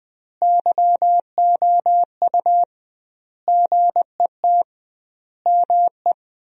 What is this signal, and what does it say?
Morse code, heard as a keyed tone, about 21 wpm, 700 Hz. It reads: YOU GET ME
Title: 2x Word-Spacing - Sentences from Top 200 Words - 20wpm